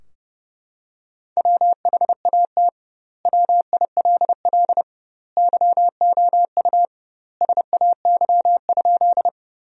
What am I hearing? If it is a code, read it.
Morse code, heard as a keyed tone, about 32 wpm, 700 Hz. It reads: WHAT WILL YOU SAY?
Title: 2x Word-Spacing - Sentences from Top 200 Words - 30wpm